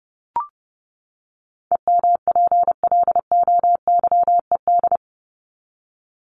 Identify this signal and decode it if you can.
Morse code, heard as a keyed tone, about 30 wpm, 700 Hz. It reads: EMPLOYED
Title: ED Suffix - Top 100 Words with Base Words 30wpm